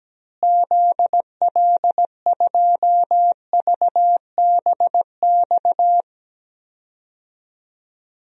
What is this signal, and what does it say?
Morse code, heard as a keyed tone, about 17 wpm, 700 Hz. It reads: ZL2VBX
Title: DXCC-100 Call Signs - 6 Characters - Repeated 2x - 17wpm